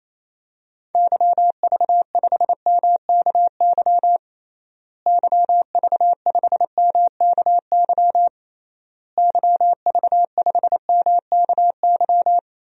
Morse code, heard as a keyed tone, about 28 wpm, 700 Hz. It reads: YV5MKY YV5MKY YV5MKY
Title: DXCC-100 Call Signs - 6 Characters - Repeated 3x - 28wpm